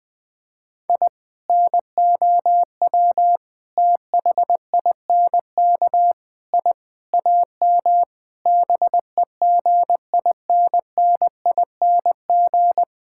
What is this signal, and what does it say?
Morse code, heard as a keyed tone, about 20 wpm, 700 Hz. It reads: I NOW THINK I AM BEGINNING
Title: Sentences from Top 1000 Words - Review Part 5 - 20wpm